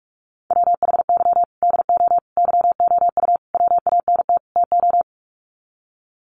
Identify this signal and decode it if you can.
Morse code, heard as a keyed tone, about 44 wpm, 700 Hz. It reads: WHY DO YOU WANT TO
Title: Sets of 5 Words Encore2 45wpm